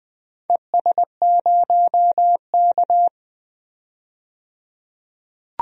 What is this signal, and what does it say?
Morse code, heard as a keyed tone, about 20 wpm, 700 Hz. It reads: ES0K
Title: DXCC-100 Call Signs - 4 Characters - 20wpm